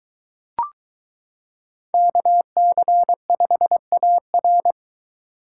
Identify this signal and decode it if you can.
Morse code, heard as a keyed tone, about 23 wpm, 700 Hz. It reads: KC5AR